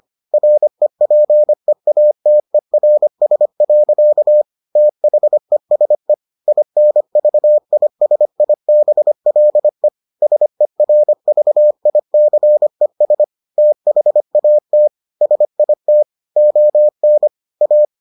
Morse code, heard as a keyed tone, about 25 wpm, 600 Hz. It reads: REPEATERS. THESE INVISIBLE SERVICES THAT SIT ON A